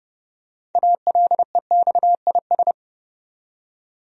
Morse code, heard as a keyed tone, about 30 wpm, 700 Hz. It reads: ALEXIS
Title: USA Common Names - 30wpm